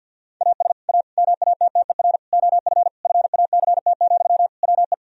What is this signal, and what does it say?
Morse code, heard as a keyed tone, about 50 wpm, 700 Hz. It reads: AS A MATTER OF FACT, WE
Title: Sentences from Top 1000 Words - Review Part 3 - 50wpm